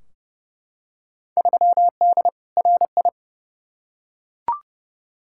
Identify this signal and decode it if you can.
Morse code, heard as a keyed tone, about 30 wpm, 700 Hz. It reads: 3D RI
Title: Field Day Exchanges Encore - 30wpm